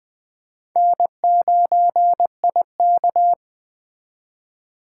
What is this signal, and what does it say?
Morse code, heard as a keyed tone, about 20 wpm, 700 Hz. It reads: N9IK
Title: Speed Racing - USA Call Signs 17wpm